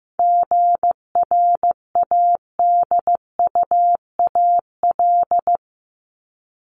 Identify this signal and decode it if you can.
Morse code, heard as a keyed tone, about 15 wpm, 700 Hz. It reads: GRADUAL
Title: LY Suffix - Top 100 Words with Base Words 15wpm